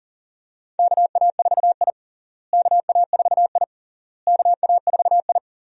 Morse code, heard as a keyed tone, about 40 wpm, 700 Hz. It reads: KA4I KA4I KA4I
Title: USA Call Signs - 4 Characters - Repeated 3x - 40wpm